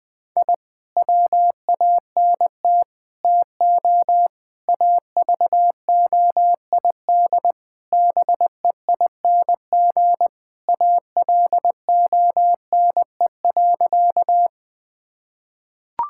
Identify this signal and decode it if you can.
Morse code, heard as a keyed tone, about 20 wpm, 700 Hz. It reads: I WANT TO AVOID BEING ALONE.
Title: Sentences from Top 1000 Words - Review Part 6 - 20wpm